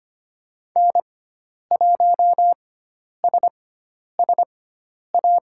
Morse code, heard as a keyed tone, about 25 wpm, spaced as for 10 wpm, 700 Hz. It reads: N1SSA
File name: Lesson 16g - Letter c - Call Signs 25wpm